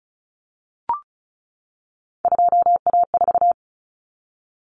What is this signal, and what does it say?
Morse code, heard as a keyed tone, about 34 wpm, 700 Hz. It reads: 2A4
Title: Common Cut Numbers - 3-Digit 35wpm